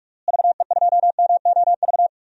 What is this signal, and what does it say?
Morse code, heard as a keyed tone, about 41 wpm, 700 Hz. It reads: VE2MOV
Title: DXCC-100 Call Signs - 6 Characters - Repeated 3x - 45wpm